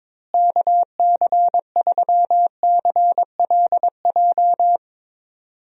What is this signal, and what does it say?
Morse code, heard as a keyed tone, about 22 wpm, 700 Hz. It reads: KC3CLJ
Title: USA Call Signs - 6 Characters - Repeated 3x - 22wpm